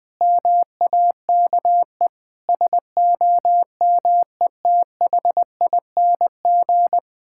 Morse code, heard as a keyed tone, about 20 wpm, 700 Hz. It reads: MAKE SOMETHING
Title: Sentences from Top 1000 Words - Review Part 5 - 20wpm